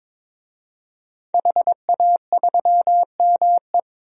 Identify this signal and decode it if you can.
Morse code, heard as a keyed tone, about 21 wpm, 700 Hz. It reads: HA3ME
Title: DXCC-100 Call Signs - 5 Characters - 22wpm